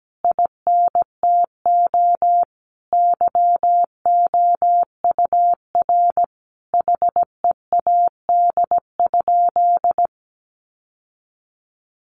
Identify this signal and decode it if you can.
Morse code, heard as a keyed tone, about 17 wpm, 700 Hz. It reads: INTO YOUR HEAD?
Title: Sentences from Top 600 Words 17wpm